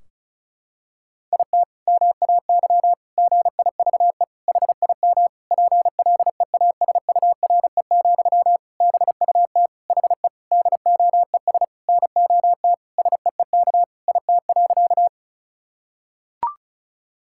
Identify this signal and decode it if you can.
Morse code, heard as a keyed tone, about 36 wpm, 700 Hz. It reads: IT MAY GIVE HIM PLEASURE, BUT HE DOES NOT SEEK IT.
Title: Sentences from Top 1600 Words 35wpm